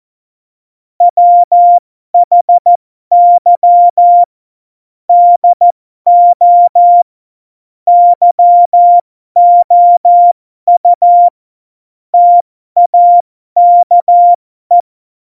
Morse code, heard as a keyed tone, about 14 wpm, spaced as for 12 wpm, 700 Hz. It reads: WHY DO YOU TAKE
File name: Morsekurs / Witze